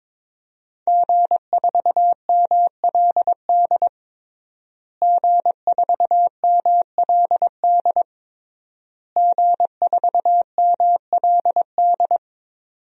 Morse code, heard as a keyed tone, about 22 wpm, 700 Hz. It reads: G4MLD G4MLD G4MLD
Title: DXCC-100 Call Signs - 5 Characters - Repeated 3x - 22wpm